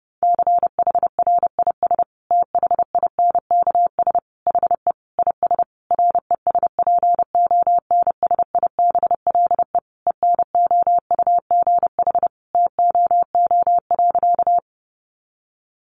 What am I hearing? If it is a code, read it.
Morse code, heard as a keyed tone, about 30 wpm, 700 Hz. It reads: CHRIS THINKS HE IS RESPONSIBLE ENOUGH TOO.
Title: Additional Easy Phrases 30wpm